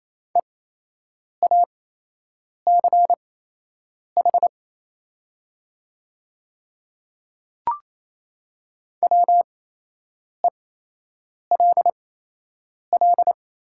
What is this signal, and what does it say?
Morse code, heard as a keyed tone, about 28 wpm, spaced as for 8 wpm, 700 Hz. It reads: EACH WELL
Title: 8x Character Spacing - Top 300 Words - 28wpm